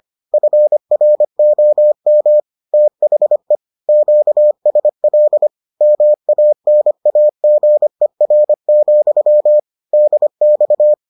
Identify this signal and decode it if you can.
Morse code, heard as a keyed tone, about 25 wpm, 600 Hz. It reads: FROM THE QSL MANAGER, DX